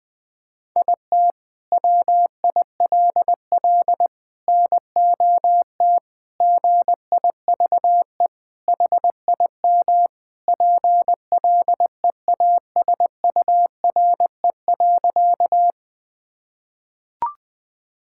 Morse code, heard as a keyed tone, about 20 wpm, 700 Hz. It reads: IT WILL NOT GIVE HIM PLEASURE.
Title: Sentences from Top 1600 Words 20wpm